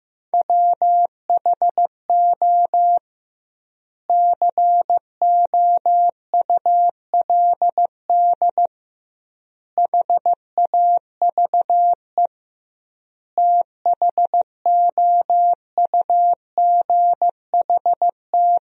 Morse code, heard as a keyed tone, about 15 wpm, 700 Hz. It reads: WHO COULD HAVE THOUGHT
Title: 2x Word-Spacing - Sentences from Top 300 Words - 15wpm